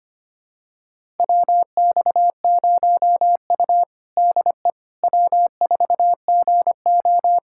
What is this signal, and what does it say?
Morse code, heard as a keyed tone, about 25 wpm, 700 Hz. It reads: WX0U DE W4GO
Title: USA Callsign DE Callsign - 4 Characters - 25wpm